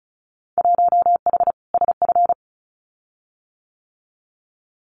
Morse code, heard as a keyed tone, about 35 wpm, 700 Hz. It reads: 1H SF
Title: Winter Field Day Exchanges 35wpm